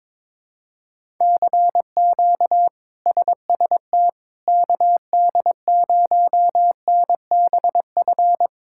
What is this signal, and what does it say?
Morse code, heard as a keyed tone, about 22 wpm, 700 Hz. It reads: CQ SST KD0NBF
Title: K1USN SST QSO Practice - 22wpm